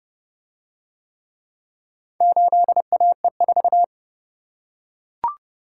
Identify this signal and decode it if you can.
Morse code, heard as a keyed tone, about 28 wpm, 700 Hz. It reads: 8AE4